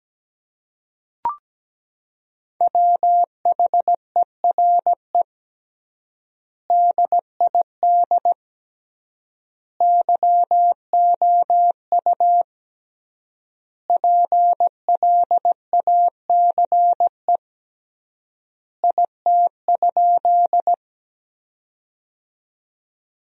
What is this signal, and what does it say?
Morse code, heard as a keyed tone, about 17 wpm, 700 Hz. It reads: WHERE DID YOU PLACE IT?
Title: 3x Word-Spacing - Sentences from Top 200 Words - 17wpm